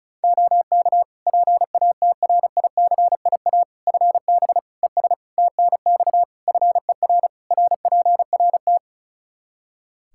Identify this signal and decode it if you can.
Morse code, heard as a keyed tone, about 36 wpm, 700 Hz. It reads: OK PATRICIA FB ES TNX FER RPRT
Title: Practice QSO - Exchange 2 - Info - 35wpm